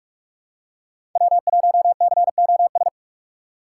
Morse code, heard as a keyed tone, about 46 wpm, 700 Hz. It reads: W1COS
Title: USA Call Signs - 5 Characters - 3x Character Spacing - 45wpm